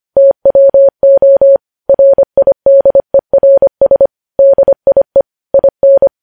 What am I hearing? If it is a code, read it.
Morse code, heard as a keyed tone, about 25 wpm, 550 Hz. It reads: TWO RIDERS DIE IN
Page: News Headlines in Morse Code at 25 WPM